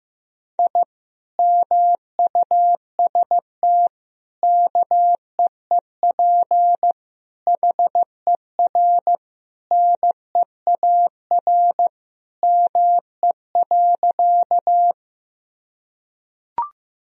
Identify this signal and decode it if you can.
Morse code, heard as a keyed tone, about 16 wpm, 700 Hz. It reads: I MUST KEEP HER NEAR ME.
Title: Sentences from Top 1000 Words - Review Part 1 - 15wpm